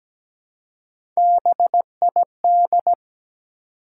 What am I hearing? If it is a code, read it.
Morse code, heard as a keyed tone, about 17 wpm, 700 Hz. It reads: BID
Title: Three Letter Words - 17wpm